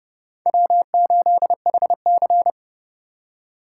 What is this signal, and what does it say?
Morse code, heard as a keyed tone, about 30 wpm, 700 Hz. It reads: W8HC
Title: USA Call Signs - 4x Character Spacing - 30wpm